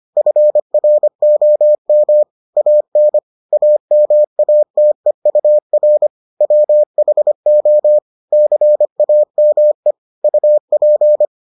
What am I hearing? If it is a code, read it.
Morse code, heard as a keyed tone, about 25 wpm, 600 Hz. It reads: FROM AN AMATEUR WHO CAME UP